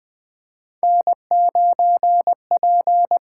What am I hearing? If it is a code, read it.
Morse code, heard as a keyed tone, about 20 wpm, 700 Hz. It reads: N9P
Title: USA Call Signs - 5 Characters - 8x Character Spacing - 20wpm